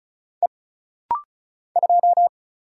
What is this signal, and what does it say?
Morse code, heard as a keyed tone, about 36 wpm, 700 Hz. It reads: E 2
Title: Single Letter-Number Alt - Rapid-Fire 35wpm